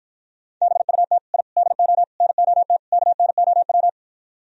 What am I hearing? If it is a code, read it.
Morse code, heard as a keyed tone, about 54 wpm, 700 Hz. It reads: BUT I DO NOT KNOW